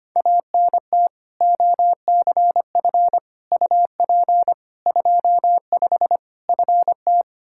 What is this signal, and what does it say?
Morse code, heard as a keyed tone, about 25 wpm, 700 Hz. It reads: ANT OCF UP 25 FT
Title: Practice QSO - Exchange 2 - Info - 25wpm